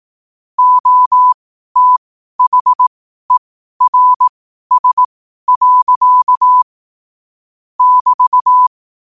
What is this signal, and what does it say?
Morse code, heard as a keyed tone, about 18 wpm, spaced as for 12 wpm, 1000 Hz. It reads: OTHERS. <BT>
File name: qotdfiles / 12wpm